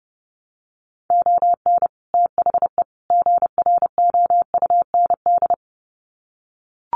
Morse code, heard as a keyed tone, about 30 wpm, 700 Hz. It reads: ON THE GROUND
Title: Sets of 3 Words 30wpm